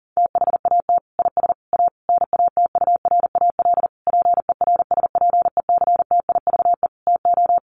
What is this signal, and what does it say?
Morse code, heard as a keyed tone, about 40 wpm, 700 Hz. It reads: THAT IS A NATURAL PERSPECTIVE TO
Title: Sentences from Top 1200 Words 40wpm